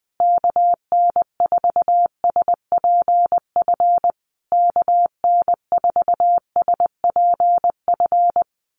Morse code, heard as a keyed tone, about 20 wpm, 700 Hz. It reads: KN4SPF KN4SPF
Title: USA Call Signs Repeated 3x 20wpm